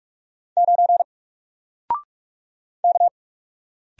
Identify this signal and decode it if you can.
Morse code, heard as a keyed tone, about 44 wpm, 700 Hz. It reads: 9 K
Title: Single Character (Letter:Number:Punctuation:Prosign) 45wpm